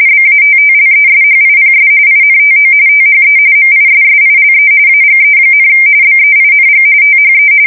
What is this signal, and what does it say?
RTTY, about 45.45 baud. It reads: BATTLESHIPS INTO THE MISSILE AGE, BEING THE
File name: fsk170 / battleshipnj